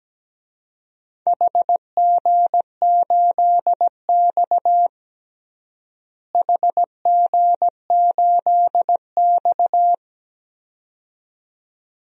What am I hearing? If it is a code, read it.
Morse code, heard as a keyed tone, about 17 wpm, 700 Hz. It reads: HG8X HG8X
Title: DXCC-100 Call Signs - 4 Characters - Repeated 2x - 17wpm